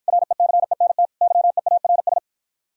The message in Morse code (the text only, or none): RECENT YEARS